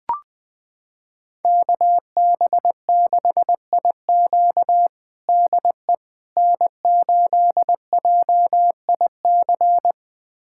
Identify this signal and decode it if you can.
Morse code, heard as a keyed tone, about 20 wpm, 700 Hz. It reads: KB6IQ DE N8JIC
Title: USA Callsign DE Callsign - 5 Characters - 20wpm